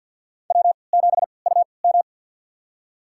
Morse code, heard as a keyed tone, about 46 wpm, spaced as for 27 wpm, 700 Hz. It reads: W7UM
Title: USA Call Signs - 3x Character Spacing - 50wpm